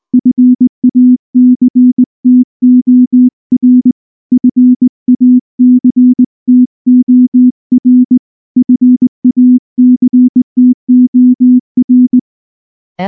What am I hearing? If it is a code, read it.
Morse code, heard as a keyed tone, about 19 wpm, 250 Hz. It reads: FACTOR FACTOR FACTOR
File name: morse / morse-training-20WPM-4